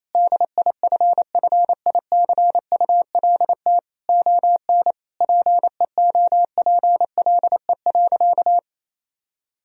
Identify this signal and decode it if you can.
Morse code, heard as a keyed tone, about 28 wpm, 700 Hz. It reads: DIFFICULT ON PEOPLE.
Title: Sentences from Top 1000 Words - Review Part 3 - 28wpm